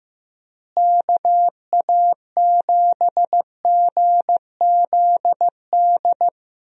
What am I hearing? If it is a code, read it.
Morse code, heard as a keyed tone, about 15 wpm, 700 Hz. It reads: KA7GZD
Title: USA Call Signs - 6 Characters - Repeated 2x - 15wpm